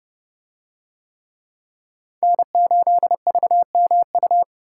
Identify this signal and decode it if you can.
Morse code, heard as a keyed tone, about 31 wpm, 700 Hz. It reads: N8VMU